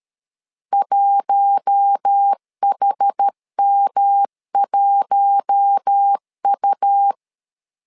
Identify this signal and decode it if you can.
Morse code, heard as a keyed tone, about 13 wpm, 800 Hz. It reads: 1HM1U
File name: LEZIONI / 4 Q1 12wpm 2028